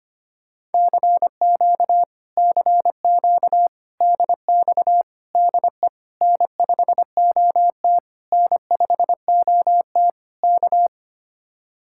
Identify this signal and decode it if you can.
Morse code, heard as a keyed tone, about 30 wpm, 700 Hz. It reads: CQ CQ DX DE N5OT N5OT K
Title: Speed-Racing Common QSO Phrases - 25wpm